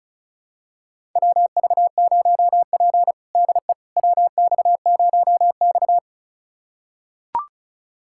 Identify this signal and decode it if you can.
Morse code, heard as a keyed tone, about 36 wpm, 700 Hz. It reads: WV0P DE WX0X
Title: USA Callsign DE Callsign - 4 Characters - 35wpm